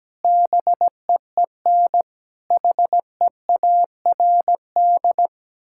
Morse code, heard as a keyed tone, about 17 wpm, 700 Hz. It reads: BEEN HEARD
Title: Sentences from Top 1600 Words 17wpm